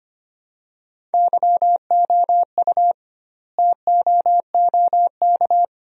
Morse code, heard as a keyed tone, about 25 wpm, 700 Hz. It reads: YOU TOOK
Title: 2x Word-Spacing - Sentences from Top 300 Words - 25wpm